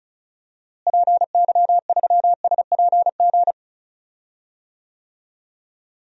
Morse code, heard as a keyed tone, about 36 wpm, 700 Hz. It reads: PY3SPG
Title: DXCC-100 Call Signs - 6 Characters - Repeated 2x - 35wpm